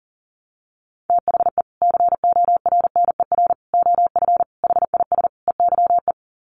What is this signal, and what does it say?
Morse code, heard as a keyed tone, about 37 wpm, 700 Hz. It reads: THE CORNER OF HIS EYE
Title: Sets of 5 Words Encore2 40wpm